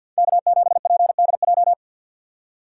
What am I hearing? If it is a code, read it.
Morse code, heard as a keyed tone, about 47 wpm, 690 Hz. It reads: K7PDJ
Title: USA Call Signs - 5 Characters - 4x Character Spacing - 50wpm